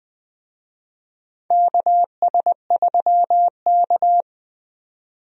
Morse code, heard as a keyed tone, about 20 wpm, 700 Hz. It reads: KS3K
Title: USA Call Signs - 4 Characters - 20wpm